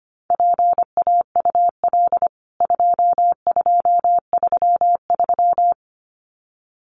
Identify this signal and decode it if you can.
Morse code, heard as a keyed tone, about 25 wpm, 700 Hz. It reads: PAUL 2233
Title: CWT Exchange 25wpm